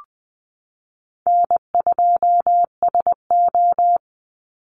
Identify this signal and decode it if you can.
Morse code, heard as a keyed tone, about 20 wpm, 700 Hz. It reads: N2SO
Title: Easy Call Signs Repeated 2x - 20wpm